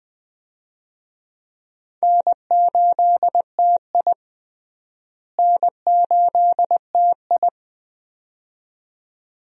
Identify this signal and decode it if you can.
Morse code, heard as a keyed tone, about 20 wpm, 700 Hz. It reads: N8TI N8TI
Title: USA Call Signs - 4 Characters - Repeated 2x - 20wpm